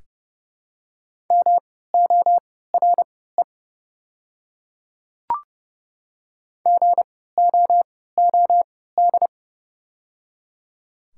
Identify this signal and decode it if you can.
Morse code, heard as a keyed tone, about 30 wpm, spaced as for 17 wpm, 690 Hz. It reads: MORE GOOD
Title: 3x Character Spacing - Top 200 Words - 30wpm